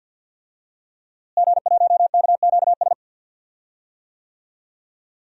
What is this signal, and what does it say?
Morse code, heard as a keyed tone, about 49 wpm, 700 Hz. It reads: G1KQS
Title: DXCC-100 Call Signs - 5 Characters - Repeated 2x - 50wpm